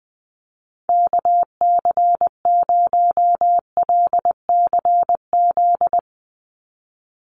Morse code, heard as a keyed tone, about 20 wpm, 700 Hz. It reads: KC0LCZ
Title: USA Call Signs 20wpm